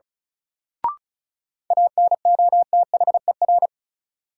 Morse code, heard as a keyed tone, about 36 wpm, 700 Hz. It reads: ANOTHER
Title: Sets of 5 Words Encore - New Word Drill - 35wpm